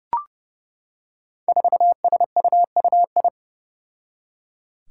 Morse code, heard as a keyed tone, about 32 wpm, 700 Hz. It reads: 4SUUI
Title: Five Challenging Letter-Number Combinations 30wpm